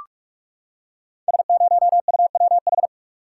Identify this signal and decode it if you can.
Morse code, heard as a keyed tone, about 44 wpm, 690 Hz. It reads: S0UWH
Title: Code Groups - 5 Letter:Numbers - 45wpm